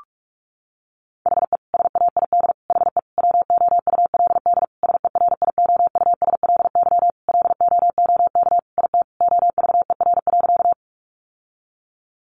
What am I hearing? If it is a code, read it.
Morse code, heard as a keyed tone, about 44 wpm, 700 Hz. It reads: HE SAID HE WOULD SERIOUSLY LOOK IT OVER.
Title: Sentences from Top 1200 Words 45wpm